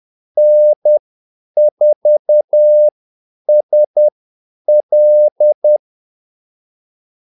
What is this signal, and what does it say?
Morse code, heard as a keyed tone, about 10 wpm, spaced as for 8 wpm, 600 Hz. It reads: N4SL
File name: Callsigns-10-8wpm